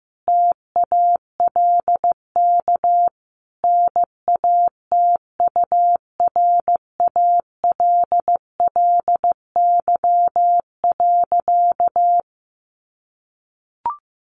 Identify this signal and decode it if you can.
Morse code, heard as a keyed tone, about 14 wpm, 700 Hz. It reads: TALK NATURALLY.
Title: Sentences from Top 1300 Words 15wpm